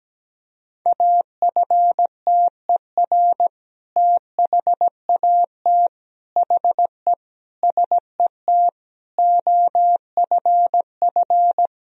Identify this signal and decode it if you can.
Morse code, heard as a keyed tone, about 17 wpm, 700 Hz. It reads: AFTER THAT HE SET OFF
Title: Sentences from Top 1000 Words - Review Part 1 - 17wpm